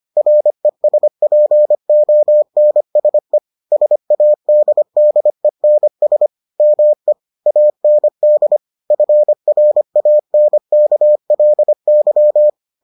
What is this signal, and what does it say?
Morse code, heard as a keyed tone, about 25 wpm, 600 Hz. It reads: RESPONSE SADDENS ME AND FRANKLY